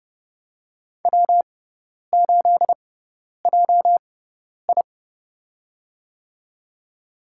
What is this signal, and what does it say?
Morse code, heard as a keyed tone, about 30 wpm, spaced as for 10 wpm, 700 Hz. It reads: W8JI
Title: USA Call Signs - 4 Characters - 6x Character Spacing - 30wpm